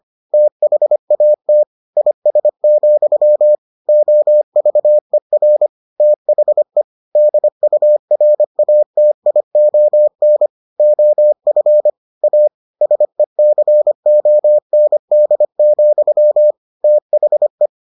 Morse code, heard as a keyed tone, about 25 wpm, 600 Hz. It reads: THAT IS, OVER THE DURATION OF A SECOND, THE